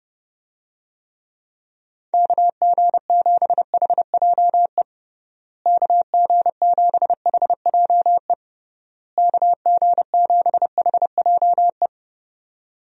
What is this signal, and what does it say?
Morse code, heard as a keyed tone, about 30 wpm, 700 Hz. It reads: KG7HJE KG7HJE KG7HJE
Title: USA Call Signs - 6 Characters - Repeated 3x - 30wpm